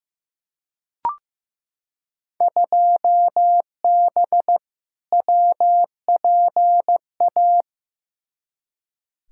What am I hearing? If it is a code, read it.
Morse code, heard as a keyed tone, about 15 wpm, 700 Hz. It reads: 2B WPA
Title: Field Day Exchanges Encore - 15wpm